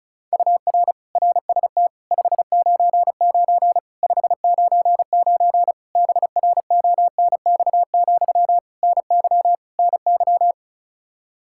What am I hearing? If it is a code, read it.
Morse code, heard as a keyed tone, about 36 wpm, 700 Hz. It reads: UR RST 599 599 BRONX, NY NY
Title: Practice QSO - Exchange 1 - Beginning - 35wpm